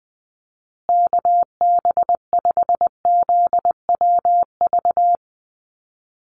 Morse code, heard as a keyed tone, about 20 wpm, 700 Hz. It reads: KB5ZWV
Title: USA Call Signs Repeated 3x 20wpm